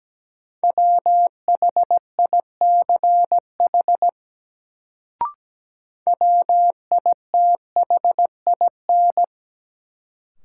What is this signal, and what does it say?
Morse code, heard as a keyed tone, about 17 wpm, 700 Hz. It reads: WHICH WITHIN
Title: Sets of 2 Words - New Word Drill - 17wpm